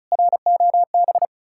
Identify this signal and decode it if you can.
Morse code, heard as a keyed tone, about 40 wpm, 700 Hz. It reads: ROB
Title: Speed-Racing Common QSO Phrases - 35wpm